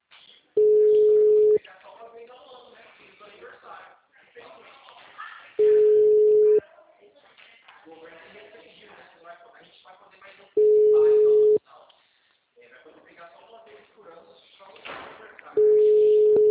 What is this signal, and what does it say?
Ringback tone (Europe, 425 Hz), with speech.